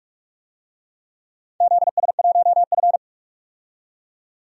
Morse code, heard as a keyed tone, about 46 wpm, 700 Hz. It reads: ZS1F
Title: DXCC-100 Call Signs - 4 Characters - 45wpm